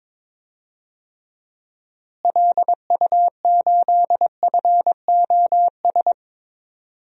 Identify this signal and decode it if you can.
Morse code, heard as a keyed tone, about 22 wpm, 700 Hz. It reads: LU8FOS
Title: DXCC-100 Call Signs - 6 Characters - Repeated 3x - 22wpm